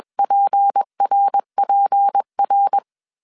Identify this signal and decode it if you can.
Morse code, heard as a keyed tone, about 21 wpm, 800 Hz. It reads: PRPR
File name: CW / DET / 1 6 1500C S C 20wpm 2116